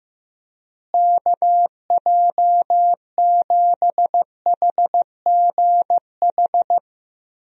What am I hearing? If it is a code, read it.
Morse code, heard as a keyed tone, about 15 wpm, 700 Hz. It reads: KJ7HGH
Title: USA Call Signs - 6 Characters - Repeated 2x - 15wpm